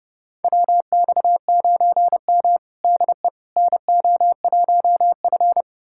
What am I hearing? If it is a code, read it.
Morse code, heard as a keyed tone, about 30 wpm, 700 Hz. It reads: WX9M DE NO1F
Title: USA Callsign DE Callsign - 4 Characters - 30wpm